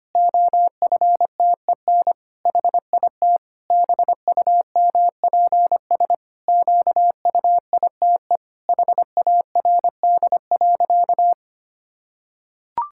Morse code, heard as a keyed tone, about 25 wpm, 700 Hz. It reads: OFTEN HIT BUMPS QUITE HARD.
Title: Pangram Challenge v2 - 25wpm